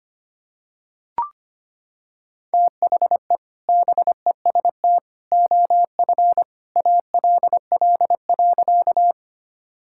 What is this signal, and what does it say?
Morse code, heard as a keyed tone, about 25 wpm, 700 Hz. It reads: THE BEST OF ALL.
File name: practice / 25wpm / Sentences with 2-Word Phrases - 25wpm